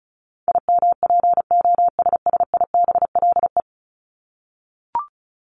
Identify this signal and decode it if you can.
Morse code, heard as a keyed tone, about 34 wpm, 700 Hz. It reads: IMPOSSIBLE
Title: Top 1001-1250 Words 35wpm